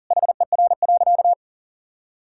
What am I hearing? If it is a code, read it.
Morse code, heard as a keyed tone, about 40 wpm, 700 Hz. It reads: HER.
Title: Sentences from Top 1000 Words - Review Part 5 - 40wpm